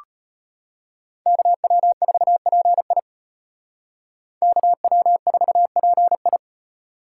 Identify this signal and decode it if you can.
Morse code, heard as a keyed tone, about 32 wpm, 700 Hz. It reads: KW4PI KW4PI